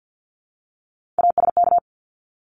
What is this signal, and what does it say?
Morse code, heard as a keyed tone, about 50 wpm, 700 Hz. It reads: ASK
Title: ING Suffix - Top 100 Words with Base Words 50wpm